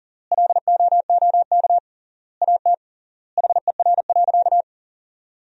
Morse code, heard as a keyed tone, about 40 wpm, 700 Hz. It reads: LOOK AT HER.
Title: 3x Word-Spacing - Sentences from Top 200 Words - 40wpm